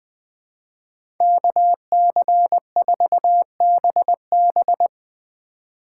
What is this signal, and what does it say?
Morse code, heard as a keyed tone, about 20 wpm, 700 Hz. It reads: KC4BB
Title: 2023 POTA Activator Call Signs - 20wpm